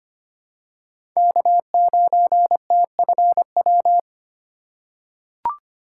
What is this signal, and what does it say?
Morse code, heard as a keyed tone, about 25 wpm, 700 Hz. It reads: K9TFW
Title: USA Call Signs - 5 Characters - 3x Character Spacing - 25wpm